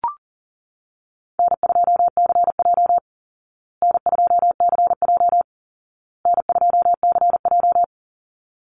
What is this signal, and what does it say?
Morse code, heard as a keyed tone, about 40 wpm, 700 Hz. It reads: N2CJ N2CJ N2CJ
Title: Easy Call Signs Repeated 3x - 40wpm